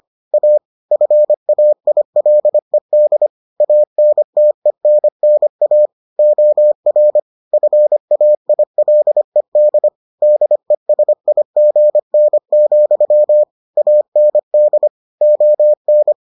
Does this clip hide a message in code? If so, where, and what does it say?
Yes: Morse code from 0.3 to 16.1 s, heard as a keyed tone, about 25 wpm, 600 Hz. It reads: A FAILED ANTENNA OR FAILED DESIGN, AND ON